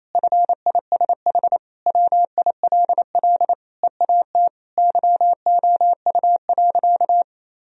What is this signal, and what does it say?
Morse code, heard as a keyed tone, about 28 wpm, 700 Hz. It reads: FISH WILL EAT YOU.
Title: Sentences from Top 1000 Words - Review Part 1 - 28wpm